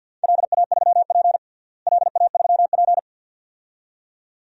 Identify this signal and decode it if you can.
Morse code, heard as a keyed tone, about 46 wpm, 700 Hz. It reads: LA3P LA3P
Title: DXCC-100 Call Signs - 4 Characters - Repeated 2x - 50wpm